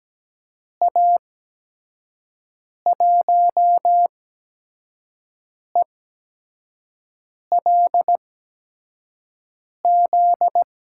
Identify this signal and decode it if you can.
Morse code, heard as a keyed tone, about 17 wpm, spaced as for 5 wpm, 700 Hz. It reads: A1ELZ
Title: USA Call Signs - 6 Characters - 8x Character Spacing - 17wpm